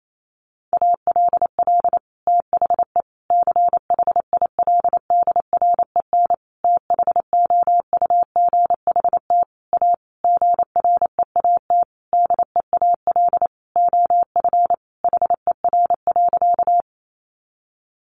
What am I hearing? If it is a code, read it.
Morse code, heard as a keyed tone, about 28 wpm, 700 Hz. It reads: ALL THE CHILDREN THOUGHT A GREAT DEAL OF HER.
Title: Sentences from Top 800 Words 28wpm